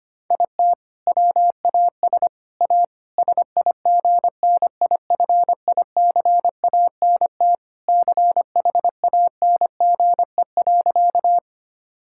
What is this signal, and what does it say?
Morse code, heard as a keyed tone, about 25 wpm, 700 Hz. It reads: IT WAS A SIGNIFICANT CHANGE.
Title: Sentences from Top 1000 Words - Review Part 6 - 25wpm